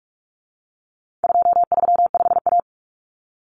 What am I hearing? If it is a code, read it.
Morse code, heard as a keyed tone, about 44 wpm, 700 Hz. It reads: 235A
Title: Common Cut Numbers - 4-Digit 45wpm